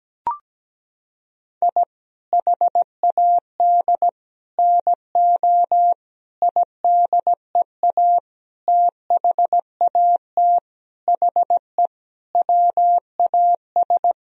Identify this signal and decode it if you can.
Morse code, heard as a keyed tone, about 17 wpm, 700 Hz. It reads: I HAD NO IDEA THAT HE WAS
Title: Sentences from Top 1000 Words - Review Part 2 - 17wpm